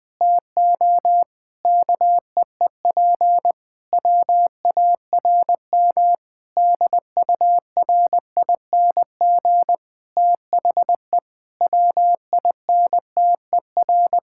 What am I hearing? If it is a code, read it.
Morse code, heard as a keyed tone, about 20 wpm, 700 Hz. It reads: TO KEEP WARM DURING THE WINTER
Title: Difficult Sentences with Hints - 20wpm